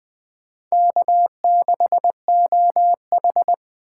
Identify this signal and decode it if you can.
Morse code, heard as a keyed tone, about 22 wpm, 700 Hz. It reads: K6OH